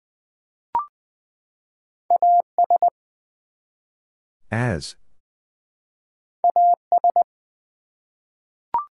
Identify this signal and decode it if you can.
Morse code, heard as a keyed tone, about 20 wpm, 700 Hz. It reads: AS AS
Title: Two Letter Words - 20wpm